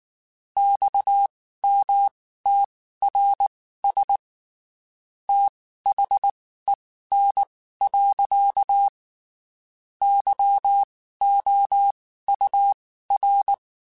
Morse code, heard as a keyed tone, about 19 wpm, spaced as for 14 wpm, 800 Hz. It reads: XMTRS THEN. YOUR
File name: (spaced) / QSOs Spaced-mp3s / WA7QOX 1 @18 sp